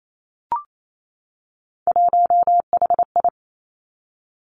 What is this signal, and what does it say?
Morse code, heard as a keyed tone, about 28 wpm, 700 Hz. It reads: 1HI
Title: Three Challenging Letter-Number Combinations 28wpm